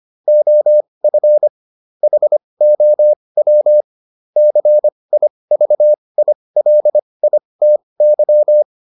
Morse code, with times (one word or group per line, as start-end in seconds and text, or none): 0.3-1.5 OF
2.0-3.8 HOW
4.4-8.6 CIVILITY